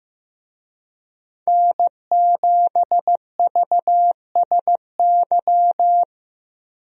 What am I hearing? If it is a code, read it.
Morse code, heard as a keyed tone, about 15 wpm, 700 Hz. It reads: N7VSY
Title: USA Call Signs - 5 Characters - Repeated 3x - 15wpm